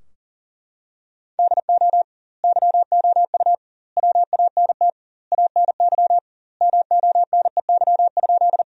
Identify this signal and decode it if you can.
Morse code, heard as a keyed tone, about 37 wpm, 700 Hz. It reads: DO YOU WANT ANY MONEY?
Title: 2x Word-Spacing - Sentences from Top 400 Words - 40wpm